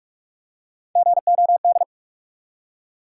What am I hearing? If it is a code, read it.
Morse code, heard as a keyed tone, about 42 wpm, 700 Hz. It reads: GOD
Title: Three Letter Words - 45wpm